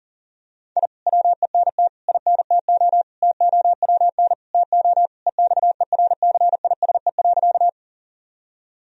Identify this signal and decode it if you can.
Morse code, heard as a keyed tone, about 40 wpm, 700 Hz. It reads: I WENT INTO TOWN TO EXERCISE.
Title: Sentences from Top 1000 Words - Review Part 4 - 40wpm